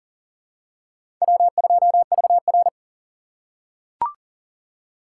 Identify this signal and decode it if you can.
Morse code, heard as a keyed tone, about 40 wpm, 700 Hz. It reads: W2VR
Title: USA Call Signs - 4 Characters - 4x Character Spacing - 40wpm